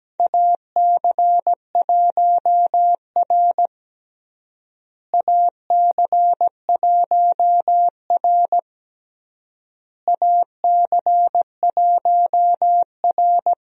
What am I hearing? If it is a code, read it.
Morse code, heard as a keyed tone, about 17 wpm, 700 Hz. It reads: AC1R AC1R AC1R
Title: USA Call Signs - 4 Characters - Repeated 3x - 17wpm